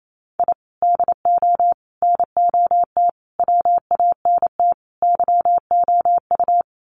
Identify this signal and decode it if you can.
Morse code, heard as a keyed tone, about 28 wpm, 700 Hz. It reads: I DO NOT WANT YOU
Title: Sets of 5 Words 28wpm